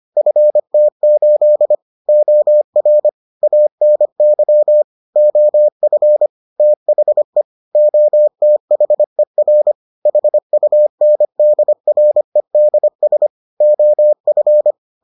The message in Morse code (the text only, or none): FT8 OR ANY OF THE OTHER HUNDREDS OF